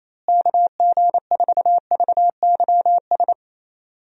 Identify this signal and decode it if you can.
Morse code, heard as a keyed tone, about 28 wpm, 700 Hz. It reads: KG4VYS
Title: USA Call Signs - 6 Characters - Repeated 2x - 28wpm